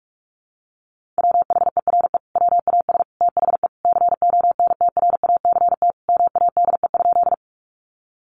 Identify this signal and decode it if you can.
Morse code, heard as a keyed tone, about 44 wpm, 700 Hz. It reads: WHERE WAS THE CONTRACT MADE?
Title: Sentences from Top 1400 Words 45wpm